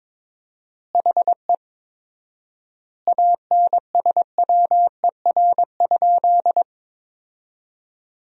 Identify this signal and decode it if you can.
Morse code, heard as a keyed tone, about 22 wpm, 700 Hz. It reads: HE ANSWER?
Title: 4x Word-Spacing - Sentences from Top 300 Words - 22wpm